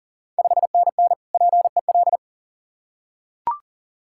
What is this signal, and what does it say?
Morse code, heard as a keyed tone, about 40 wpm, 690 Hz. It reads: 5NN PEL
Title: Ontario QSO Party Exchange 40wpm